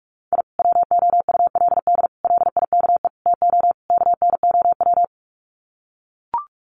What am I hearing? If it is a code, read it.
Morse code, heard as a keyed tone, about 44 wpm, 700 Hz. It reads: I WOULD LIKE TO KNOW
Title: Sets of 5 Words Encore 45wpm